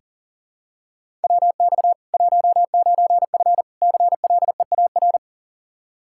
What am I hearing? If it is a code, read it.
Morse code, heard as a keyed tone, about 40 wpm, 700 Hz. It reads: WX 19F CLEAR
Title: Practice QSO - Exchange 2 - Info - 40wpm